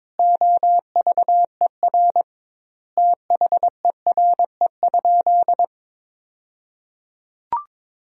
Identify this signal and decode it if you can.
Morse code, heard as a keyed tone, about 22 wpm, 700 Hz. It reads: OVER THERE?
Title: 2x Word-Spacing - Sentences from Top 200 Words - 22wpm